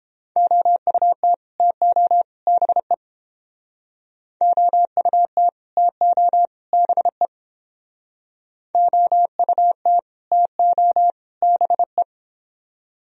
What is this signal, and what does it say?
Morse code, heard as a keyed tone, about 30 wpm, 700 Hz. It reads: OUT TO BE OUT TO BE OUT TO BE